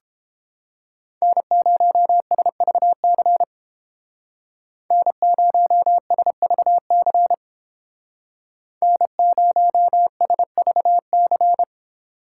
Morse code, heard as a keyed tone, about 29 wpm, 700 Hz. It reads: N0SVC N0SVC N0SVC